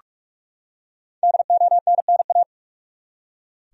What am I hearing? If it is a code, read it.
Morse code, heard as a keyed tone, about 41 wpm, 700 Hz. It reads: DONNA
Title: USA Common Names - 45wpm